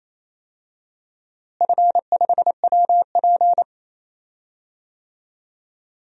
Morse code, heard as a keyed tone, about 28 wpm, 700 Hz. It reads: F5WP
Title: DXCC-100 Call Signs - 4 Characters - Repeated 3x - 28wpm